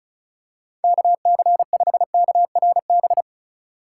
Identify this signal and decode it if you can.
Morse code, heard as a keyed tone, about 36 wpm, 700 Hz. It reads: KC5KRB
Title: USA Call Signs - 6 Characters - 2x Character Spacing - 35wpm